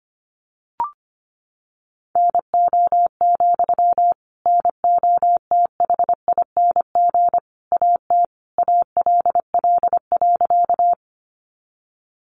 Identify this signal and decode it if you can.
Morse code, heard as a keyed tone, about 25 wpm, 700 Hz. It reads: NO, NOTHING AT ALL.
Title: Sentences from Top 500 Words 25wpm